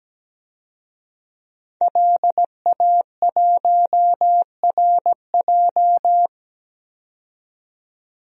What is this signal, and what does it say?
Morse code, heard as a keyed tone, about 17 wpm, 700 Hz. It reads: LA1RJ
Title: DXCC-100 Call Signs - 5 Characters - 17wpm